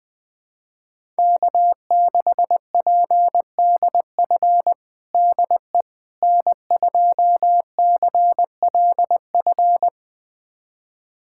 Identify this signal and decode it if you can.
Morse code, heard as a keyed tone, about 20 wpm, 700 Hz. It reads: K6PDF DE N2CLF
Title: USA Callsign DE Callsign - 5 Characters - 20wpm